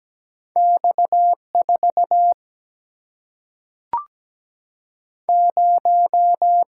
Morse code, heard as a keyed tone, about 17 wpm, 700 Hz. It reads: X4 0
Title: Single-Letter Single-Number 17wpm